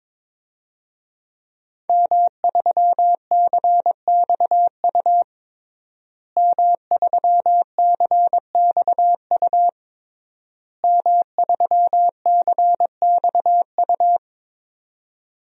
Morse code, heard as a keyed tone, about 22 wpm, 700 Hz. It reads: M3CXU M3CXU M3CXU
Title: DXCC-100 Call Signs - 5 Characters - Repeated 3x - 22wpm